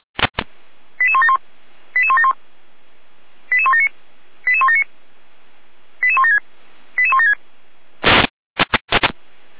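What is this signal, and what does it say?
Five-tone selective call (ZVEI), heard as quick runs of five tones, tones of about 70 ms. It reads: 88171 88171 88179 88179 88176 88176
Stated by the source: Five-tone ZVEI selective call received on an SDR.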